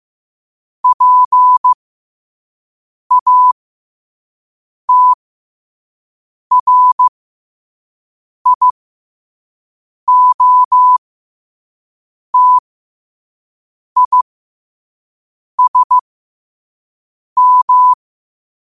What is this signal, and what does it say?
Morse code, heard as a keyed tone, about 15 wpm, spaced as for 5 wpm, 1000 Hz. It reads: PATRIOTISM